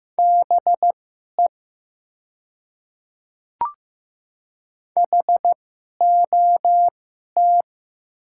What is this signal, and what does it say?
Morse code, heard as a keyed tone, about 15 wpm, spaced as for 11 wpm, 700 Hz. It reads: BE HOT
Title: 2x Character Spacing - Top 100 Words - 15wpm